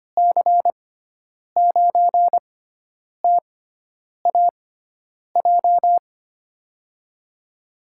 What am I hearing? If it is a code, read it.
Morse code, heard as a keyed tone, about 25 wpm, spaced as for 9 wpm, 700 Hz. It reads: C9TAJ
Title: USA Call Signs - 6 Characters - 6x Character Spacing - 25wpm